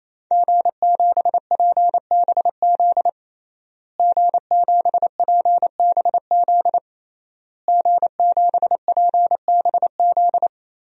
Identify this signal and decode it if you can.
Morse code, heard as a keyed tone, about 28 wpm, 700 Hz. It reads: G7PBZ G7PBZ G7PBZ
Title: DXCC-100 Call Signs - 5 Characters - Repeated 3x - 28wpm